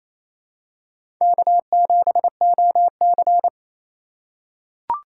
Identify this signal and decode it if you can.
Morse code, heard as a keyed tone, about 28 wpm, 700 Hz. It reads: K7OC
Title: USA Call Signs - 4 Characters - 4x Character Spacing - 28wpm